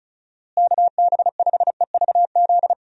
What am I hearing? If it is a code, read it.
Morse code, heard as a keyed tone, about 36 wpm, 700 Hz. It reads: KB5EVZ
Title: USA Call Signs - 6 Characters - Repeated 2x - 35wpm